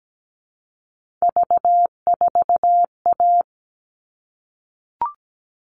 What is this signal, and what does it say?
Morse code, heard as a keyed tone, about 17 wpm, 700 Hz. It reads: V4A
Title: All Cut Numbers - 3-Digit 17wpm